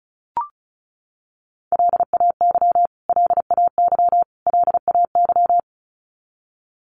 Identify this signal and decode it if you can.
Morse code, heard as a keyed tone, about 34 wpm, 700 Hz. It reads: LAY LAY LAY
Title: Top 500 Words Repeated 35wpm